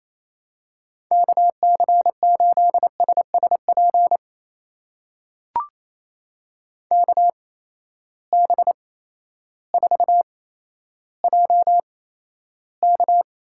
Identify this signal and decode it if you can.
Morse code, heard as a keyed tone, about 28 wpm, spaced as for 13 wpm, 700 Hz. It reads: KC8SSP KB4JK
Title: USA Call Signs - 6 Characters - 8x Character Spacing - 28wpm